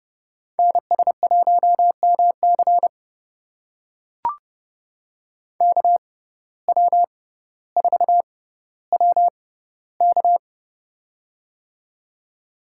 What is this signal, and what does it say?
Morse code, heard as a keyed tone, about 30 wpm, spaced as for 15 wpm, 700 Hz. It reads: NS1MC KW4WK
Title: USA Call Signs - 5 Characters - 6x Character Spacing - 30wpm